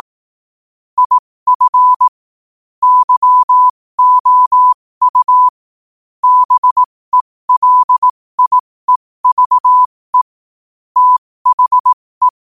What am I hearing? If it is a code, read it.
Morse code, heard as a keyed tone, about 18 wpm, spaced as for 15 wpm, 1000 Hz. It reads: IF YOU BELIEVE THE
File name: qotdfiles / 15wpm